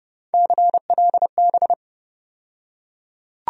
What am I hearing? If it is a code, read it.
Morse code, heard as a keyed tone, about 30 wpm, 700 Hz. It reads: CLB
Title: USA Call Sign Postfixes - 3 Letters 30wpm